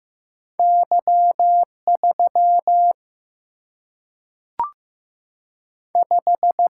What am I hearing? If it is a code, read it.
Morse code, heard as a keyed tone, about 15 wpm, 700 Hz. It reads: Y 3 5
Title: Single-Letter Single-Number 15wpm